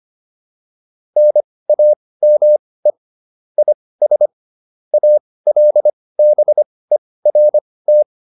Morse code, heard as a keyed tone, about 25 wpm, spaced as for 18 wpm, 600 Hz. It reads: NAME IS ALBERT